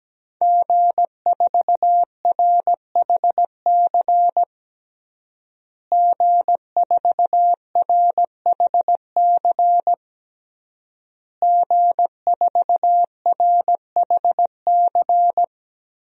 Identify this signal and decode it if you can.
Morse code, heard as a keyed tone, about 17 wpm, 700 Hz. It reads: G4RHC G4RHC G4RHC
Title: DXCC-100 Call Signs - 5 Characters - Repeated 3x - 17wpm